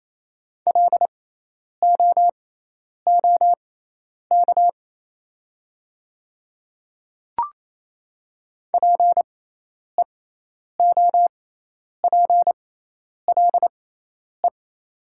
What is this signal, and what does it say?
Morse code, heard as a keyed tone, about 28 wpm, spaced as for 10 wpm, 700 Hz. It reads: LOOK PEOPLE
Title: 6x Character Spacing - Top 100 Words - 28wpm